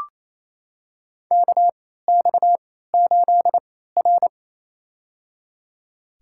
Morse code, heard as a keyed tone, about 28 wpm, spaced as for 16 wpm, 700 Hz. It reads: KX8R
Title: USA Call Signs - 4 Characters - 3x Character Spacing - 28wpm